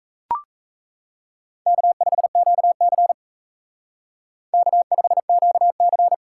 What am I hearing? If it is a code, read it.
Morse code, heard as a keyed tone, about 35 wpm, 700 Hz. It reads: K5QC K5QC